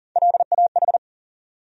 Morse code, heard as a keyed tone, about 40 wpm, 700 Hz. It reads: LAH
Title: USA Call Sign Postfixes - 3 Letters 40wpm